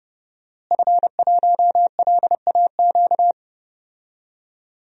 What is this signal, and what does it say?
Morse code, heard as a keyed tone, about 30 wpm, 700 Hz. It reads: F1LAQ
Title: DXCC-100 Call Signs - 5 Characters - Repeated 2x - 30wpm